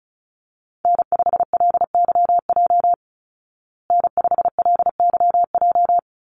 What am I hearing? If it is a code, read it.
Morse code, heard as a keyed tone, about 34 wpm, 700 Hz. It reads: N5LYJ N5LYJ
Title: Call Signs Repeated 2x - 35wpm